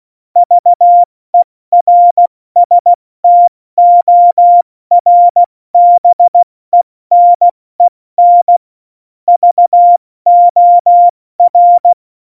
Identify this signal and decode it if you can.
Morse code, heard as a keyed tone, about 16 wpm, 700 Hz. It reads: VERSTORBENEN VOR
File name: Morsekurs / Witze / Witz 18. Februar 2020, 70 F80